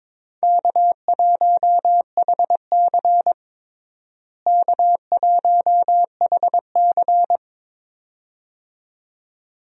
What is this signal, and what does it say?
Morse code, heard as a keyed tone, about 22 wpm, 700 Hz. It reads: K1HC K1HC
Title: USA Call Signs - 4 Characters - Repeated 2x - 22wpm